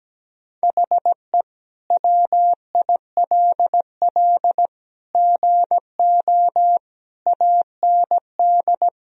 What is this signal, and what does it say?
Morse code, heard as a keyed tone, about 17 wpm, 700 Hz. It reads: HE WILL GO AND
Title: Sentences from Top 1000 Words - Review Part 1 - 17wpm